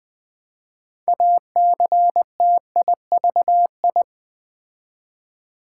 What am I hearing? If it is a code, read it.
Morse code, heard as a keyed tone, about 20 wpm, 690 Hz. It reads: ACTIVI
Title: Build A Word - 3 Letters At A Time - 8-Letter Words - 20wpm